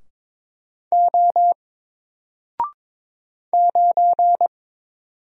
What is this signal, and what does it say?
Morse code, heard as a keyed tone, about 22 wpm, 700 Hz. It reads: O 9
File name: Single-Letter-Number-22wpm